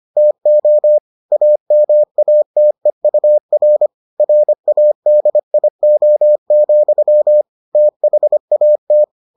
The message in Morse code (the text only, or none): TO AMATEUR RADIO, THAT